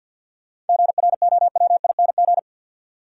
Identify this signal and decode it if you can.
Morse code, heard as a keyed tone, about 46 wpm, 700 Hz. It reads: GROWING
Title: Sets of 5 Words Encore - New Word Drill - 50wpm